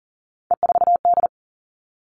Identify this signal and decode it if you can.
Morse code, heard as a keyed tone, about 40 wpm, 700 Hz. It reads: E4D
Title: All Cut Numbers - 3-Digit 40wpm